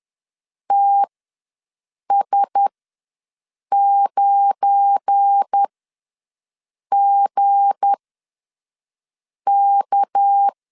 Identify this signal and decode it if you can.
Morse code, heard as a keyed tone, about 10 wpm, spaced as for 5 wpm, 800 Hz. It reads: TS9GK
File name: LEZIONI / 7 GF9Z / 7 GF9 ws 10wpm 4034